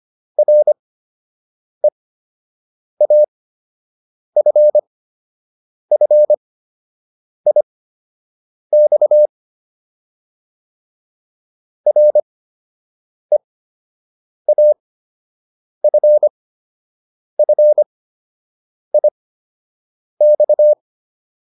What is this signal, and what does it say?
Morse code, heard as a keyed tone, about 25 wpm, spaced as for 7 wpm, 600 Hz. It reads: REAFFIX REAFFIX